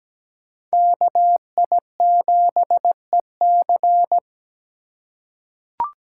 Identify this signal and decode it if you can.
Morse code, heard as a keyed tone, about 17 wpm, 700 Hz. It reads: KI7EC
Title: USA Call Signs - 6x Character Spacing - 17wpm